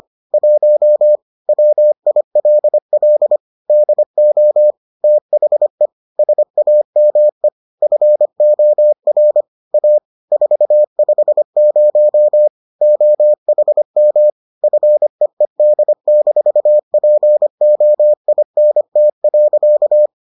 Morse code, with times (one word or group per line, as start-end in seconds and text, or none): 0.3-1.2 1
1.5-3.4 WILL
3.7-4.7 DO
5.0-5.9 THE
6.2-7.5 SAME
7.8-9.4 FOR
9.7-10.0 A
10.3-12.5 450
12.8-14.3 OHM
14.6-20.1 FEED-POINT.